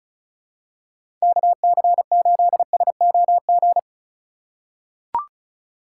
Morse code, heard as a keyed tone, about 34 wpm, 700 Hz. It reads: KC8SOG
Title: USA Call Signs - 6 Characters - 2x Character Spacing - 35wpm